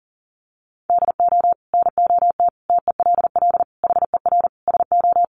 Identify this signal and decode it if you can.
Morse code, heard as a keyed tone, about 40 wpm, 700 Hz. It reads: DO NOT TELL HER SO
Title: Sentences from Top 1500 Words 40wpm